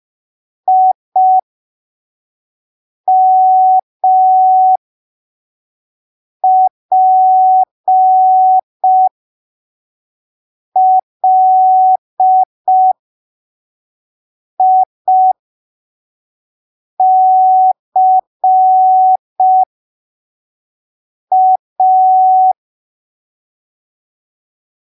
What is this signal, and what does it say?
Morse code, heard as a keyed tone, about 5 wpm, spaced as for 3 wpm, 750 Hz. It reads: IMPLICA
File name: arquivos / texto / texto 06 05ppm